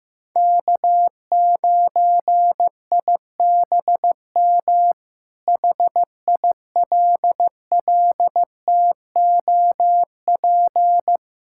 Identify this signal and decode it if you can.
Morse code, heard as a keyed tone, about 15 wpm, 700 Hz. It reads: K9IBM HILLTOP
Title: K1USN SST QSO Practice - 15wpm